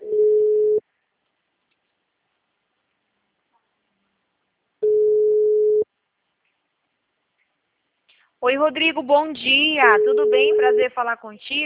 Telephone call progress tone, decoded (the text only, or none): ringback tone